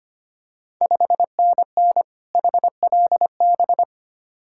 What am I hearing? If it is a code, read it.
Morse code, heard as a keyed tone, about 25 wpm, 690 Hz. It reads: 5NN HLB
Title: Ontario QSO Party Exchange 25wpm